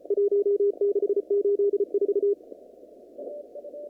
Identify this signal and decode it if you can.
Morse code, heard as a keyed tone, about 34 wpm, 400 Hz. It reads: 1684
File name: CW / kt5j wpx cw-op-k5pi